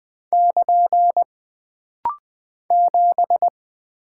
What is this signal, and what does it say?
Morse code, heard as a keyed tone, about 20 wpm, 700 Hz. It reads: <KN> 7
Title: Single Character (Letter:Number:Punctuation:Prosign) - Rapid-Fire - 20wpm